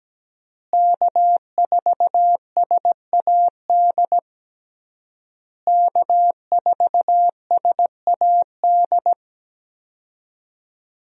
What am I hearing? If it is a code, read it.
Morse code, heard as a keyed tone, about 17 wpm, 700 Hz. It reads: K4SAD K4SAD
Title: USA Call Signs - 5 Characters - Repeated 3x - 17wpm